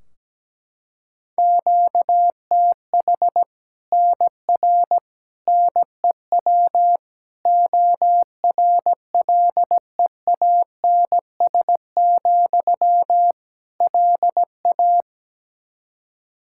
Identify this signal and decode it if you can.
Morse code, heard as a keyed tone, about 17 wpm, 700 Hz. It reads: QTH NR NEW ORLEANS, LA
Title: K1USN SST QSO Practice 17wpm